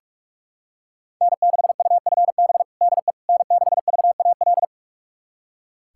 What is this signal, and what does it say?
Morse code, heard as a keyed tone, about 41 wpm, 700 Hz. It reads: N6UFB DE N6VAL
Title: USA Callsign DE Callsign - 5 Characters - 45wpm